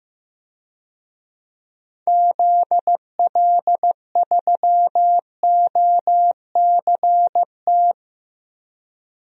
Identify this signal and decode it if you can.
Morse code, heard as a keyed tone, about 15 wpm, 700 Hz. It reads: ZL3OCT
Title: DXCC-100 Call Signs - 6 Characters - Repeated 3x - 15wpm